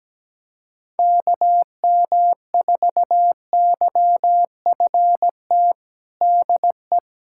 Morse code, heard as a keyed tone, about 17 wpm, 700 Hz. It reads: KM4YFT DE
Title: USA Callsign DE Callsign - 6 Characters - 17wpm